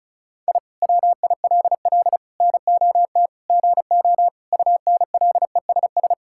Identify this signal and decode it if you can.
Morse code, heard as a keyed tone, about 36 wpm, 700 Hz. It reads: I WILL NOT GO UNLESS
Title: Sentences from Top 1000 Words - Review Part 6 - 35wpm